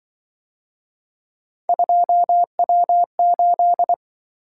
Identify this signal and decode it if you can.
Morse code, heard as a keyed tone, about 24 wpm, 700 Hz. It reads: 2W8
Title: Speed-Racing - Three Letter-Number Combinations - 20wpm